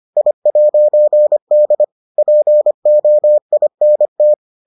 Morse code, heard as a keyed tone, about 25 wpm, 600 Hz. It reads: I'D POINT